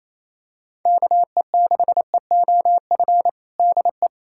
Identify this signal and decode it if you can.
Morse code, heard as a keyed tone, about 27 wpm, 700 Hz. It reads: KE6EOF DE
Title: USA Callsign DE Callsign - 6 Characters - 28wpm